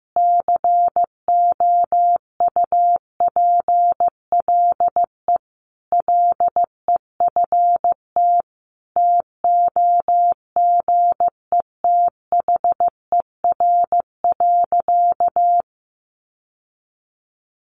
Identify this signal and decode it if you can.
Morse code, heard as a keyed tone, about 15 wpm, 700 Hz. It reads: COUPLE LEFT TOGETHER.
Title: Sentences from Top 1100 Words 15wpm